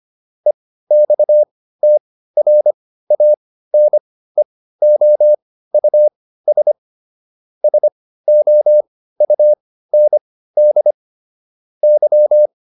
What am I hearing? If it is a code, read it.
Morse code, heard as a keyed tone, about 25 wpm, spaced as for 15 wpm, 600 Hz. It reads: EXTRANEOUS SOUND Y